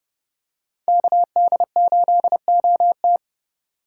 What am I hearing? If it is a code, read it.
Morse code, heard as a keyed tone, about 30 wpm, 700 Hz. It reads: KD8OT
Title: USA Call Signs - 5 Characters - 2x Character Spacing - 30wpm